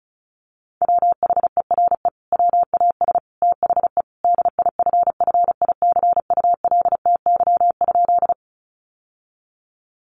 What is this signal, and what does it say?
Morse code, heard as a keyed tone, about 34 wpm, 700 Hz. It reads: WHERE WAS THE DIFFICULTY?
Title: Sentences from Top 1500 Words 35wpm